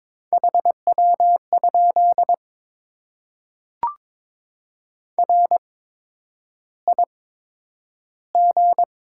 Morse code, heard as a keyed tone, about 22 wpm, spaced as for 9 wpm, 700 Hz. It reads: HW? RIG
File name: Top 20 QSO Elements - 8x Character Spacing - 22wpm